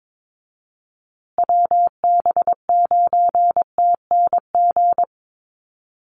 Morse code, heard as a keyed tone, about 22 wpm, 700 Hz. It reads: WB9TNG
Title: USA Call Signs Repeated 3x 22wpm